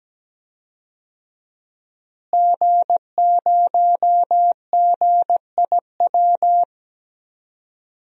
Morse code, heard as a keyed tone, about 17 wpm, 700 Hz. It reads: G0GIW
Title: DXCC-100 Call Signs - 5 Characters - Repeated 3x - 17wpm